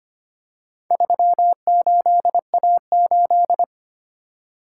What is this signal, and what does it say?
Morse code, heard as a keyed tone, about 25 wpm, 700 Hz. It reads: 38A8
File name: practice / 25wpm / Four Letter-Number Combinations - 25wpm